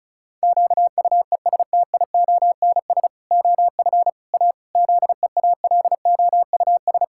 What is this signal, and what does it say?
Morse code, heard as a keyed tone, about 34 wpm, 700 Hz. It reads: QUESTIONS OF A ZEALOUS
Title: Pangram Challenge v2 - 35wpm